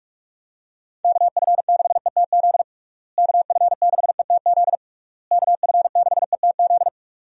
Morse code, heard as a keyed tone, about 43 wpm, 700 Hz. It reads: KF6ETZ KF6ETZ KF6ETZ
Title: USA Call Signs - 6 Characters - Repeated 3x - 45wpm